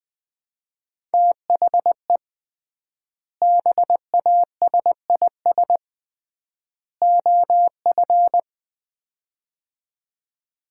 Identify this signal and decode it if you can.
Morse code, heard as a keyed tone, about 20 wpm, 700 Hz. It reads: THE BASIS OF
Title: 3x Word-Spacing - Sets of 3 Words - 20wpm